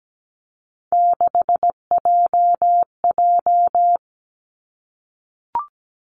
Morse code, heard as a keyed tone, about 17 wpm, 700 Hz. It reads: 6JJ
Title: Three Challenging Letter-Number Combinations 17wpm